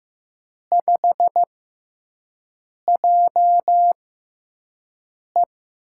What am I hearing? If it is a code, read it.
Morse code, heard as a keyed tone, about 15 wpm, spaced as for 5 wpm, 700 Hz. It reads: 5JE
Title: USA Call Signs - 5 Characters - 6x Character Spacing - 15wpm